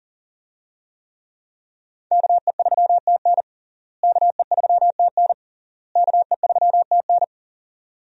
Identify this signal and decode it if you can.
Morse code, heard as a keyed tone, about 40 wpm, 700 Hz. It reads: KE3TN KE3TN KE3TN
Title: USA Call Signs - 5 Characters - Repeated 3x - 40wpm